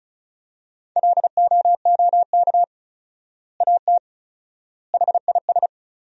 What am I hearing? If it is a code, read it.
Morse code, heard as a keyed tone, about 36 wpm, 700 Hz. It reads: LOOK AT HIS
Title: 4x Word-Spacing - Sentences from Top 300 Words - 35wpm